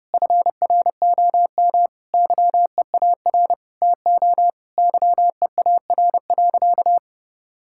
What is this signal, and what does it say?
Morse code, heard as a keyed tone, about 30 wpm, 700 Hz. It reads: FROM YEAR TO YEAR.
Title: Sentences from Top 1600 Words 30wpm